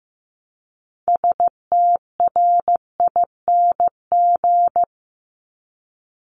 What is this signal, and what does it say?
Morse code, heard as a keyed tone, about 15 wpm, 700 Hz. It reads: STRING
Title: Multi-letter Phonemes Repeated 3x 15wpm